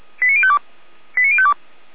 Five-tone selective call (ZVEI), heard as quick runs of five tones, tones of about 70 ms. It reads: 89052 89052